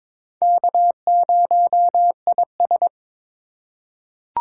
Morse code, heard as a keyed tone, about 22 wpm, 700 Hz. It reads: K0IS
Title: USA Call Signs - 4x Character Spacing - 22wpm